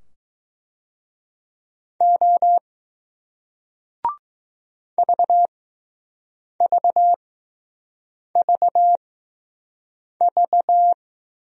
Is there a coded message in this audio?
Yes: Morse code from 2.0 to 10.9 s, heard as a keyed tone, about 20 wpm, spaced as for 6 wpm, 700 Hz. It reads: O VVVV